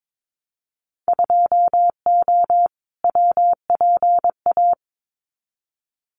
Morse code, heard as a keyed tone, about 22 wpm, 700 Hz. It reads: 2O WPA
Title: Winter Field Day Exchanges 22wpm